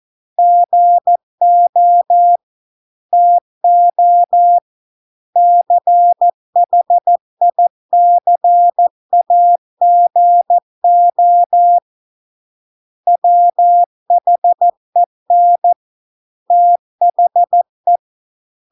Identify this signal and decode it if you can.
Morse code, heard as a keyed tone, about 14 wpm, 700 Hz. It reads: GO TO CHICAGO WHEN THE